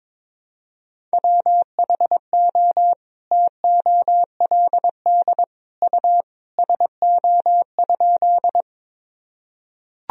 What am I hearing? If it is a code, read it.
Morse code, heard as a keyed tone, about 22 wpm, 700 Hz. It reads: WHO TOLD U SO?
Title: Sentences from Top 400 Words and Abbreviations 22wpm